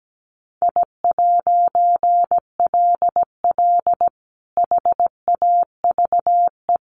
Morse code, heard as a keyed tone, about 17 wpm, 700 Hz. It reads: I'LL HAVE
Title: A Sound Of Thunder 17wpm